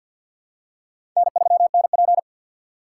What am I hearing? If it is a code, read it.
Morse code, heard as a keyed tone, about 50 wpm, 700 Hz. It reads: N3NP
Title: USA Call Signs - 4 Characters - 2x Character Spacing - 50wpm